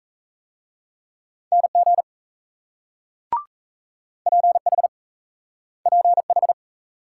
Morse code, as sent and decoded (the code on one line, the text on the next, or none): -. --. / .--. .... / .--. ....
NG PH PH